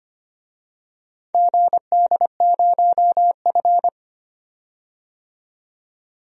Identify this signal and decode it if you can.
Morse code, heard as a keyed tone, about 25 wpm, 700 Hz. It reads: GD0F
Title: DXCC-100 Call Signs - 4 Characters - 25wpm